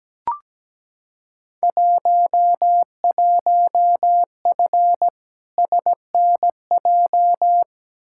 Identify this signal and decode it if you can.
Morse code, heard as a keyed tone, about 17 wpm, 700 Hz. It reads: 11F SNJ
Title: Field Day Exchanges Encore - 17wpm